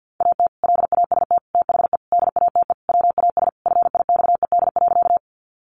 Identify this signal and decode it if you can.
Morse code, heard as a keyed tone, about 51 wpm, 700 Hz. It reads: AT LAST THE DATE WAS FIXED.
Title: Sentences from Top 1100 Words 50wpm